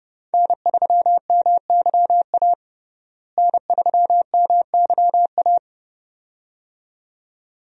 Morse code, heard as a keyed tone, about 30 wpm, 700 Hz. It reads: N3MYA N3MYA
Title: USA Call Signs - 5 Characters - Repeated 2x - 30wpm